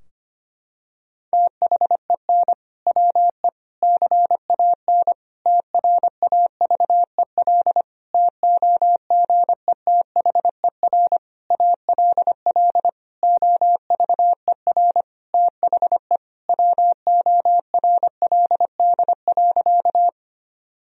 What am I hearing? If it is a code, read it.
Morse code, heard as a keyed tone, about 25 wpm, 700 Hz. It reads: THEN WE CAN TRAVEL TOGETHER ALL OVER THE WORLD.
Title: Sentences from Top 1000 Words - Review Part 2 - 25wpm